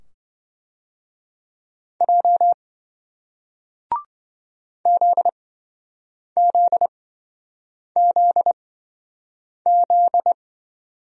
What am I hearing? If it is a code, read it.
Morse code, heard as a keyed tone, about 27 wpm, spaced as for 7 wpm, 700 Hz. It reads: J ZZZZ